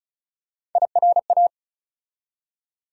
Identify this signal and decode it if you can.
Morse code, heard as a keyed tone, about 36 wpm, 700 Hz. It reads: IRA
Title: USA Call Sign Postfixes - 3 Letters 35wpm